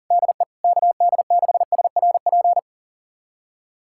Morse code, heard as a keyed tone, about 37 wpm, 700 Hz. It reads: DE KD6SRP
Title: USA Callsign DE Callsign 40wpm